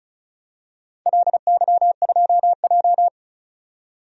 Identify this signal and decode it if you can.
Morse code, heard as a keyed tone, about 36 wpm, 700 Hz. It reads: LY2J
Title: DXCC-100 Call Signs - 4 Characters - 35wpm